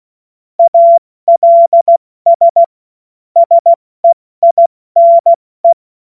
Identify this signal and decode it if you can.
Morse code, heard as a keyed tone, about 16 wpm, 680 Hz. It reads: ALS SEINE
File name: Morsekurs / Witze / Witz 11. Januar 2022, 70 F80